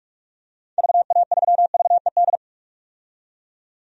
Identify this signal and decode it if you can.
Morse code, heard as a keyed tone, about 45 wpm, 700 Hz. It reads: VA3VED
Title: DXCC-100 Call Signs - 6 Characters - 45wpm